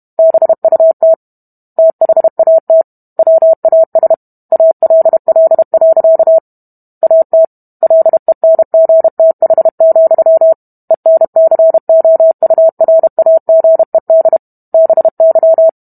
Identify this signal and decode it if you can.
Morse code, heard as a keyed tone, about 32 wpm, 650 Hz. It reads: BUT THAT WAS ALL. AT LENGTH, ENCOURAGED BY